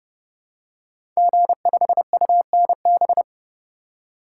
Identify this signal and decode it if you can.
Morse code, heard as a keyed tone, about 30 wpm, 700 Hz. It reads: G5UNB
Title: Code Groups - 5 Letter:Numbers - 30wpm